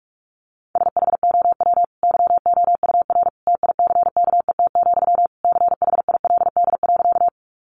Morse code, heard as a keyed tone, about 44 wpm, 700 Hz. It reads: SHOW YOUR TICKET, CHILD.
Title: Sentences from Top 1400 Words 45wpm